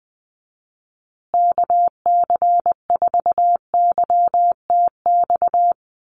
Morse code, heard as a keyed tone, about 20 wpm, 700 Hz. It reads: KC4YTX
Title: USA Call Signs Repeated 3x 20wpm